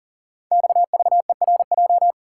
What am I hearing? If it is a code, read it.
Morse code, heard as a keyed tone, about 42 wpm, 700 Hz. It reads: XVERJ
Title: Code Groups - 5 Letters - 40wpm